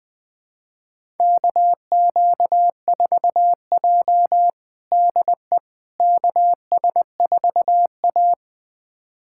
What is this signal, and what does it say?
Morse code, heard as a keyed tone, about 20 wpm, 700 Hz. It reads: KQ4J DE KS4A
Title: USA Callsign DE Callsign - 4 Characters - 20wpm